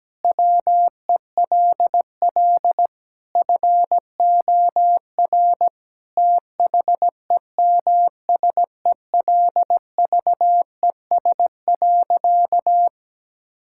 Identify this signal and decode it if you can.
Morse code, heard as a keyed tone, about 17 wpm, 700 Hz. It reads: WELL FOR THEMSELVES.
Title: Sentences from Top 1000 Words - Review Part 5 - 17wpm